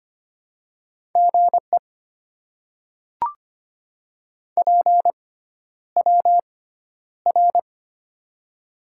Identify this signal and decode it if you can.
Morse code, heard as a keyed tone, about 25 wpm, spaced as for 11 wpm, 700 Hz. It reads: GE PWR
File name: Top 90 QSO Elements - 6x Character Spacing - 25wpm